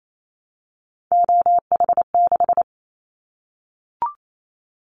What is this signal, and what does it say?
Morse code, heard as a keyed tone, about 28 wpm, 700 Hz. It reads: OH6
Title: Callsign Prefixes for Top 25 DXCC - 28wpm